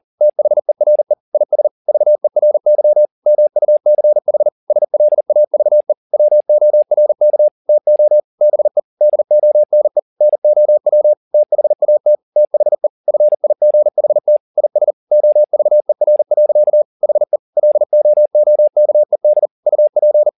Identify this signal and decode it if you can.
Morse code, heard as a keyed tone, about 40 wpm, 600 Hz. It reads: THERE IS VERY MUCH SLAVE WORK TO BE DONE NOW THAT THE FIGHT IS OVER. HE LOOKED UP
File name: the old man and the sea 2 40wpm